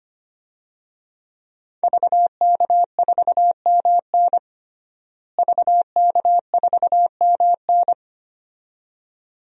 Morse code, heard as a keyed tone, about 25 wpm, 700 Hz. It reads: VK4MN VK4MN
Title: DXCC-100 Call Signs - 5 Characters - Repeated 2x - 25wpm